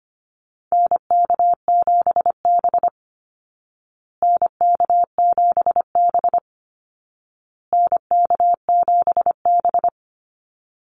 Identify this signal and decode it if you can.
Morse code, heard as a keyed tone, about 25 wpm, 700 Hz. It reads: NK7B NK7B NK7B
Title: Call Signs Repeated 2x - 25wpm